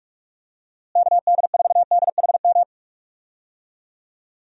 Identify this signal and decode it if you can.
Morse code, heard as a keyed tone, about 46 wpm, 700 Hz. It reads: KD4DHM
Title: USA Call Signs - 6 Characters - Repeated 2x - 45wpm